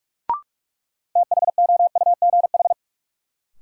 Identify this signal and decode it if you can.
Morse code, heard as a keyed tone, about 46 wpm, 700 Hz. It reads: THOUGH
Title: Sentences from Top 400 Words - New Word Drill - 45wpm